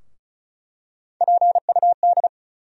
Morse code, heard as a keyed tone, about 36 wpm, 700 Hz. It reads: PUD
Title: USA Call Sign Postfixes - 3 Letters 35wpm